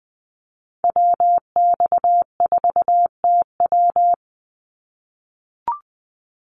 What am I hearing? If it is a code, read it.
Morse code, heard as a keyed tone, about 20 wpm, 700 Hz. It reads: WX4TW
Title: NA SOTA Activator Callsigns 20wpm